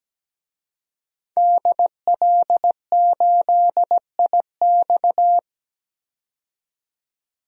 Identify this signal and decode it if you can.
Morse code, heard as a keyed tone, about 17 wpm, 700 Hz. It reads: DL8IX
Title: DXCC-100 Call Signs - 5 Characters - 17wpm